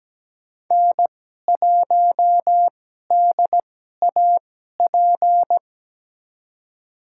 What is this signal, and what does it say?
Morse code, heard as a keyed tone, about 17 wpm, spaced as for 12 wpm, 700 Hz. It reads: N1DAP
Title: USA Call Signs - 5 Characters - 2x Character Spacing - 17wpm